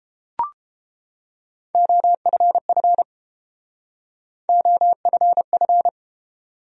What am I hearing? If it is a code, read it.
Morse code, heard as a keyed tone, about 30 wpm, 700 Hz. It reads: OFF OFF